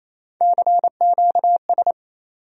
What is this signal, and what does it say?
Morse code, heard as a keyed tone, about 28 wpm, 700 Hz. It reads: CQS
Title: USA Call Sign Postfixes - 3 Letters 28wpm